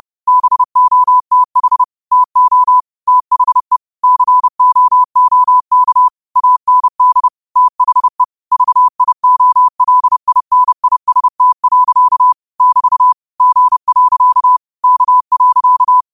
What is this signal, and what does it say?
Morse code, heard as a keyed tone, about 30 wpm, 1000 Hz. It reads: BOTH TO THE COOK AND THE VIOLINIST. <BT> G. K.
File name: qotdfiles / 30wpm